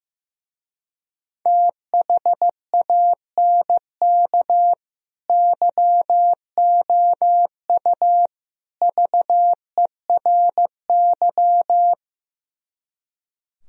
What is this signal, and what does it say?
Morse code, heard as a keyed tone, about 15 wpm, 700 Hz. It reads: THANK YOU VERY
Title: Speed-Racing - Sets of 3 Words 15wpm